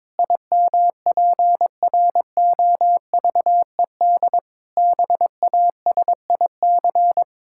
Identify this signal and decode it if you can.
Morse code, heard as a keyed tone, about 22 wpm, 700 Hz. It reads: IMPROVED BASIC
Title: Pangram Challenge v2 - 22wpm